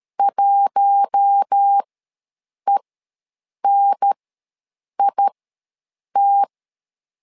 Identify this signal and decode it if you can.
Morse code, heard as a keyed tone, about 13 wpm, spaced as for 7 wpm, 800 Hz. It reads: 1ENIT
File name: LEZIONI / 4 Q1 ws 12wpm 2129